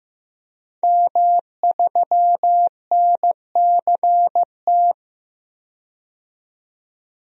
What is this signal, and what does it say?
Morse code, heard as a keyed tone, about 15 wpm, 700 Hz. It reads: M3NCT
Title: DXCC-100 Call Signs - 5 Characters - 15wpm